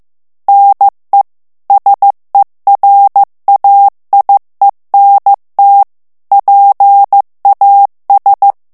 Morse code, heard as a keyed tone, about 15 wpm, 800 Hz. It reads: NE SERAIENT PAS
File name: morse mp3 / FRANCAIS TEXTE CLAIR 2 VITESSE 14 MOTS MINUTES